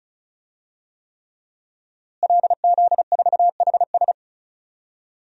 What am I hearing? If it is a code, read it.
Morse code, heard as a keyed tone, about 36 wpm, 700 Hz. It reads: LZ4HS
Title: DXCC-100 Call Signs - 5 Characters - Repeated 3x - 35wpm